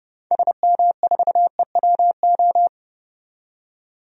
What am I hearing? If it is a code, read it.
Morse code, heard as a keyed tone, about 30 wpm, 700 Hz. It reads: SM4EWO
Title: DXCC-100 Call Signs - 6 Characters - 30wpm